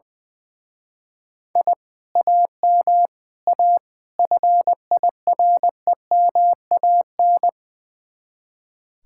Morse code, heard as a keyed tone, about 20 wpm, 700 Hz. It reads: I AM A FIREMAN
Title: Speed-Racing Common QSO Phrases - 20wpm